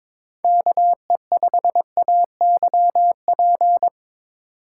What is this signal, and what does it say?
Morse code, heard as a keyed tone, about 22 wpm, 700 Hz. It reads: KE5AYP
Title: USA Call Signs - 6 Characters - 3x Character Spacing - 22wpm